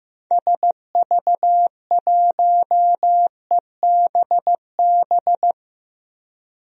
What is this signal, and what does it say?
Morse code, heard as a keyed tone, about 15 wpm, 700 Hz. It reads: SV1EBB
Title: DXCC-100 Call Signs - 6 Characters - 15wpm